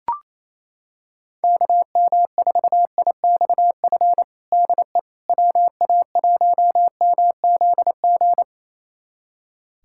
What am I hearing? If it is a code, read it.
Morse code, heard as a keyed tone, about 28 wpm, 700 Hz. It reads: KM4IXF DE WA1MZG
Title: USA Callsign DE Callsign 28wpm